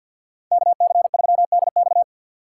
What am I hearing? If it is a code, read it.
Morse code, heard as a keyed tone, about 53 wpm, 700 Hz. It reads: KC3DX
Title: USA Call Signs - 5 Characters - 2x Character Spacing - 50wpm